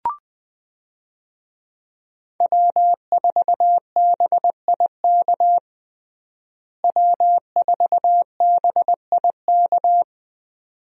Morse code, heard as a keyed tone, about 20 wpm, 700 Hz. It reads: W4BIK W4BIK
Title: USA Call Signs - 5 Characters - Repeated 2x - 20wpm